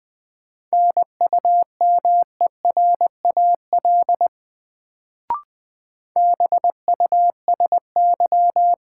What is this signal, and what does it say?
Morse code, heard as a keyed tone, about 20 wpm, 700 Hz. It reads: NUMERAL BUSY
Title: Sentences from Top 1000 Words - Review Part 2 - New Word Drill - 20wpm